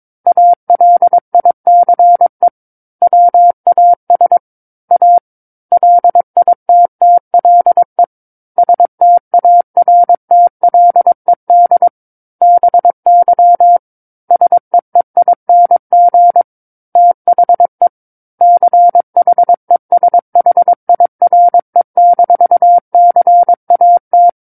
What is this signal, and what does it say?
Morse code, heard as a keyed tone, about 22 wpm, 700 Hz. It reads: ALICE WAS A LITTLE STARTLED BY SEEING THE CHESHIRE-CAT
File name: 06-Chapter-06 22 WPM